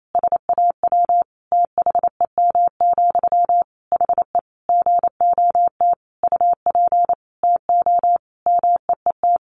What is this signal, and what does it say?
Morse code, heard as a keyed tone, about 28 wpm, 700 Hz. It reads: SAW THEM, HE GOT UP TO MEET
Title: Sentences from Top 1100 Words 28wpm